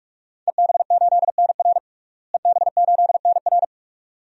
Morse code, heard as a keyed tone, about 43 wpm, 700 Hz. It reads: EB8NR EB8NR
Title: DXCC-100 Call Signs - 5 Characters - Repeated 2x - 45wpm